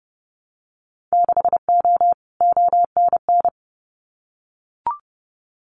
Morse code, heard as a keyed tone, about 30 wpm, 700 Hz. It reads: 6O ONN
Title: Winter Field Day Exchanges Encore 30wpm